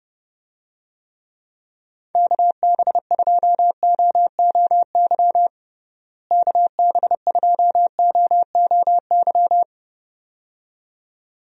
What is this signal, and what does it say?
Morse code, heard as a keyed tone, about 30 wpm, 700 Hz. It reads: KB2OOY KB2OOY
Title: USA Call Signs - 6 Characters - Repeated 2x - 30wpm